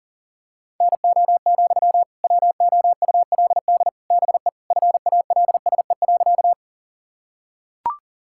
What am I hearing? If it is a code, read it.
Morse code, heard as a keyed tone, about 40 wpm, 700 Hz. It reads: NO, WOULD BE FALSE.
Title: Sentences from Top 1600 Words 40wpm